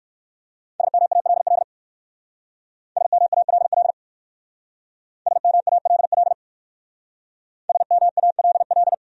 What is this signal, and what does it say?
Morse code, heard as a keyed tone, about 56 wpm, 700 Hz. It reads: SMALL SMALL SMALL SMALL